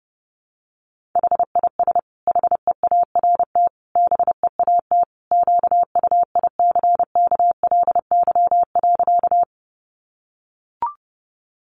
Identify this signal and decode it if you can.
Morse code, heard as a keyed tone, about 30 wpm, 700 Hz. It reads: HIS HEART BEAT QUICKLY.
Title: Sentences from Top 1300 Words 30wpm